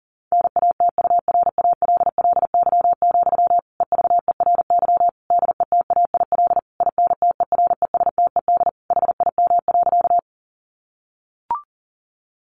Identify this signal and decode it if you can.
Morse code, heard as a keyed tone, about 37 wpm, 700 Hz. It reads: NATURALLY, EVERY DETAIL INTERESTED HIM.
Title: Sentences from Top 1500 Words 40wpm